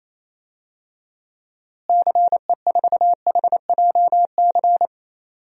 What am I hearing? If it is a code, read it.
Morse code, heard as a keyed tone, about 28 wpm, 700 Hz. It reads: CE4HJC
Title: DXCC-100 Call Signs - 6 Characters - Repeated 2x - 28wpm